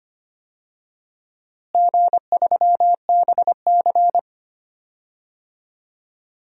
Morse code, heard as a keyed tone, about 25 wpm, 700 Hz. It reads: G3BC
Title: DXCC-100 Call Signs - 4 Characters - Repeated 2x - 25wpm